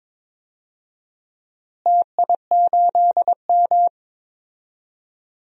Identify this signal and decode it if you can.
Morse code, heard as a keyed tone, about 22 wpm, 700 Hz. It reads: TI8M
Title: DXCC-100 Call Signs - 4 Characters - Repeated 3x - 22wpm